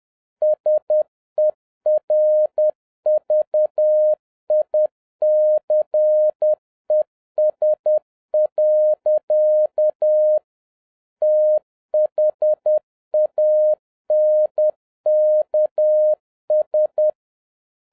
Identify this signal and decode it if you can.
Morse code, heard as a keyed tone, about 10 wpm, 610 Hz. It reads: SERVICES. THANKS